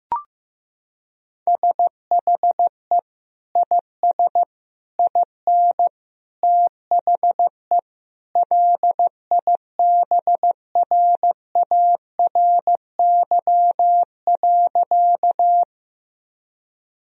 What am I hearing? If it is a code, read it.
Morse code, heard as a keyed tone, about 16 wpm, 700 Hz. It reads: SHE IS IN THE LIBRARY.
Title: Sentences from Top 1000 Words - Review Part 4 - 15wpm